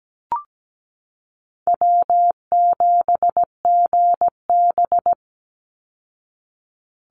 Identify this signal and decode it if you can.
Morse code, heard as a keyed tone, about 17 wpm, 700 Hz. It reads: W7GB
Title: Easy Call Signs Repeated 3x - 17wpm